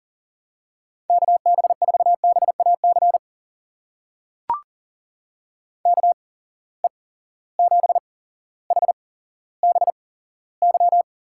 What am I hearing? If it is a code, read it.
Morse code, heard as a keyed tone, about 41 wpm, spaced as for 17 wpm, 700 Hz. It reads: KB4BAC KE7HBY
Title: USA Call Signs - 6 Characters - 8x Character Spacing - 40wpm